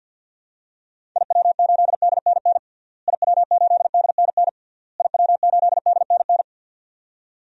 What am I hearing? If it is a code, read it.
Morse code, heard as a keyed tone, about 53 wpm, 700 Hz. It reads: IW8DNN IW8DNN IW8DNN
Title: DXCC-100 Call Signs - 6 Characters - Repeated 3x - 50wpm